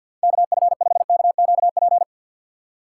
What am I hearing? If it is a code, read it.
Morse code, heard as a keyed tone, about 52 wpm, 700 Hz. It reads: KF5KQP
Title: USA Call Signs - 6 Characters - 3x Character Spacing - 50wpm